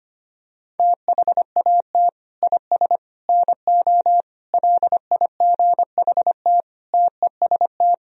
Morse code, heard as a keyed tone, about 25 wpm, 700 Hz. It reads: THAT IS NO LIGHT TEST
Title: Sentences from Top 1000 Words - Review Part 3 - 25wpm